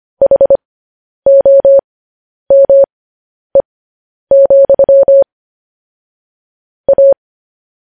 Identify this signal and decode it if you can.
Morse code, heard as a keyed tone, about 25 wpm, spaced as for 10 wpm, 550 Hz. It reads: HOME, A